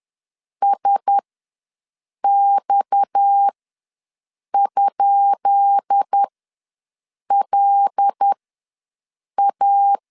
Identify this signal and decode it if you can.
Morse code, heard as a keyed tone, about 10 wpm, spaced as for 6 wpm, 800 Hz. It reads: SX?LA
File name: LEZIONI / 10 SIGNS / 10B INT ws 10wpm 4134